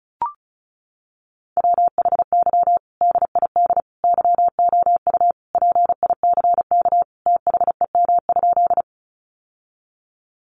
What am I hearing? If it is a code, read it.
Morse code, heard as a keyed tone, about 34 wpm, 700 Hz. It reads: WHY DID YOU PICK THEM?
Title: Sentences from Top 1000 Words 35wpm